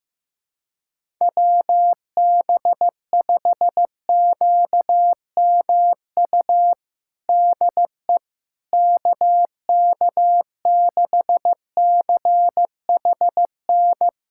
Morse code, heard as a keyed tone, about 15 wpm, 700 Hz. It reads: WB5QMU DE KK6CHN
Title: USA Callsign DE Callsign 15wpm